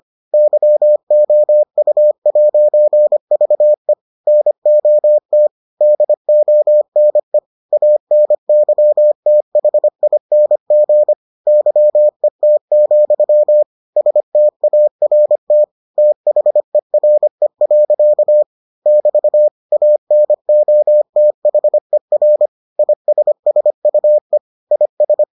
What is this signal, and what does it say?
Morse code, heard as a keyed tone, about 25 wpm, 600 Hz. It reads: YOU'VE NOT DONE ANYTHING YET, START THERE. <BT> ANOTHER ISSUE IS